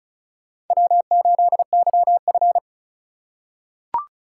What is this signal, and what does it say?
Morse code, heard as a keyed tone, about 36 wpm, 700 Hz. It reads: W8YF
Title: USA Call Signs - 4 Characters - 3x Character Spacing - 35wpm